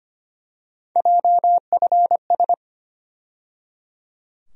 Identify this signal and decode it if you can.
Morse code, heard as a keyed tone, about 25 wpm, 700 Hz. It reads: JFS
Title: USA Call Sign Postfixes - 3 Letters 25wpm